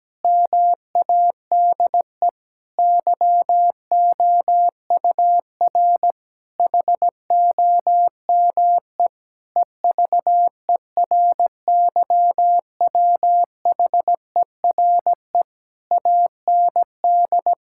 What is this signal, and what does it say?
Morse code, heard as a keyed tone, about 17 wpm, 700 Hz. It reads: MADE YOUR HOME EVERYWHERE AND
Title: Sentences from Top 1600 Words 17wpm